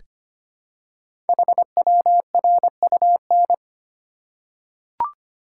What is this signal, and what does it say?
Morse code, heard as a keyed tone, about 25 wpm, 700 Hz. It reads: HWRUN
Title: Code Groups - 5 Letters - 25wpm